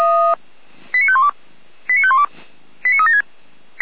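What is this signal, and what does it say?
Five-tone selective call (ZVEI), heard as quick runs of five tones, tones of about 70 ms. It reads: 89512 89512 89376
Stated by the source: ZVEI five-tone call received by SDR.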